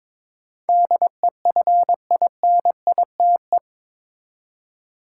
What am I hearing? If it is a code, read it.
Morse code, heard as a keyed tone, about 22 wpm, 700 Hz. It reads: DEFINITE
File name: practice / 22wpm / Eight Letter Words - 22wpm